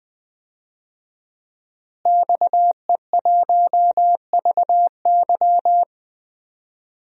Morse code, heard as a keyed tone, about 20 wpm, 700 Hz. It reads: XE1VY
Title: DXCC-100 Call Signs - 5 Characters - Repeated 2x - 20wpm